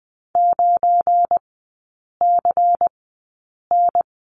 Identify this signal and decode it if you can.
Morse code, heard as a keyed tone, about 19 wpm, spaced as for 8 wpm, 700 Hz. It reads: 9CN
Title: 205 Lesson 33e - Letter j - Three Characters 20wpm